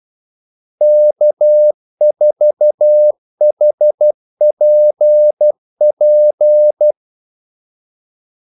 Morse code, heard as a keyed tone, about 12 wpm, 600 Hz. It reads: K4HPP
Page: 2 ekstra ordmellemrum mellem kaldesignalerne